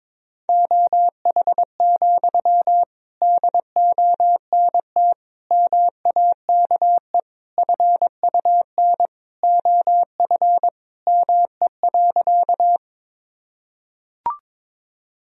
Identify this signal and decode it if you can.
Morse code, heard as a keyed tone, about 22 wpm, 700 Hz. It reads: OH, DONT MAKE FUN OF ME.
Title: Sentences from Top 1000 Words - Review Part 4 - 22wpm